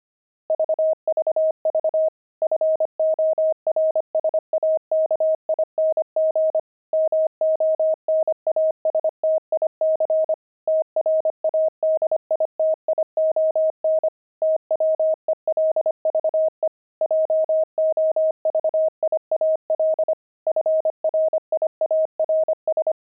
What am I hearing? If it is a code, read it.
Morse code, heard as a keyed tone, about 25 wpm, 630 Hz. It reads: VVV FORSAKING MONASTIC TRADITION TWELVE JOVIAL FRIARS
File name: pangram-fs-25-cs-25-friars